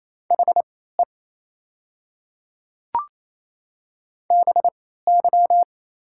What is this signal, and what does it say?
Morse code, heard as a keyed tone, about 28 wpm, spaced as for 16 wpm, 700 Hz. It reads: HE BY
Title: 3x Character Spacing - Top 200 Words - 28wpm